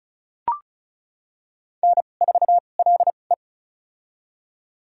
Morse code, heard as a keyed tone, about 36 wpm, spaced as for 25 wpm, 690 Hz. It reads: N4LE
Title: USA Call Signs - 4 Characters - 2x Character Spacing - 35wpm